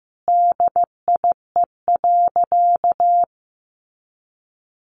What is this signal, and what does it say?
Morse code, heard as a keyed tone, about 15 wpm, 700 Hz. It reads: DIE.
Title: Sentences from Top 1300 Words 15wpm